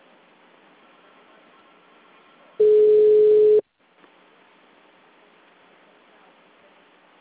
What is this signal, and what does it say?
Ringback tone (Europe, 425 Hz).